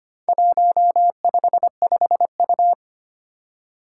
Morse code, heard as a keyed tone, about 25 wpm, 700 Hz. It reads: 155U
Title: Four Challenging Letter-Number Combinations 25wpm